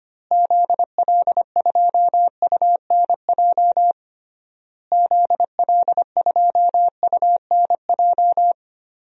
Morse code, heard as a keyed tone, about 25 wpm, 700 Hz. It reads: ZL2UNJ ZL2UNJ
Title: DXCC-100 Call Signs - 6 Characters - Repeated 3x - 25wpm